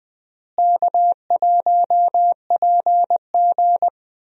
Morse code, heard as a keyed tone, about 20 wpm, 700 Hz. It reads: K1PG
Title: USA Call Signs - 4 Characters - Repeated 3x - 20wpm